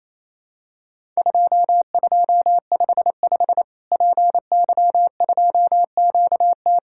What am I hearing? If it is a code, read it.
Morse code, heard as a keyed tone, about 28 wpm, 700 Hz. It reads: 2255 PY2QT
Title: Worked All Europe DX Contest - QTC Exchange - 28wpm